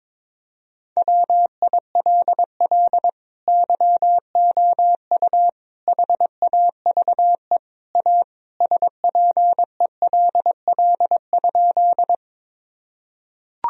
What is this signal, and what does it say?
Morse code, heard as a keyed tone, about 22 wpm, 700 Hz. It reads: WILL YOU HAVE A SPELL?
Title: Sentences from Top 1000 Words - Review Part 1 - 22wpm